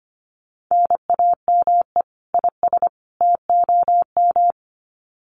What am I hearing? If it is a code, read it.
Morse code, heard as a keyed tone, about 25 wpm, 700 Hz. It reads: NAME IS TOM
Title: Common QSO Phrases 25wpm